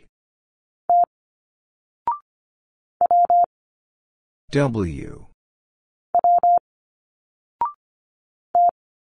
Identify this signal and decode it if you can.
Morse code, heard as a keyed tone, about 25 wpm, 700 Hz. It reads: T W W T
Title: Single Letters 25wpm